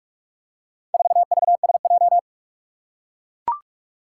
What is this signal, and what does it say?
Morse code, heard as a keyed tone, about 44 wpm, 700 Hz. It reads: 4VSJ
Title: Four Challenging Letter-Number Combinations 45wpm